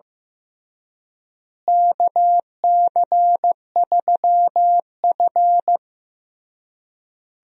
Morse code, heard as a keyed tone, about 15 wpm, 700 Hz. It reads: KC3F
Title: USA Call Signs - 4 Characters - Repeated 3x - 15wpm